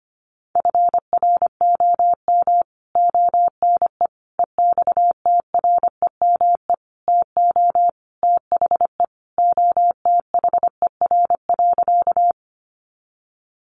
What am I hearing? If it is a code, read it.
Morse code, heard as a keyed tone, about 25 wpm, 700 Hz. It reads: FROM ONE EXTREME TO THE OTHER.
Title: Sentences from Top 1300 Words 25wpm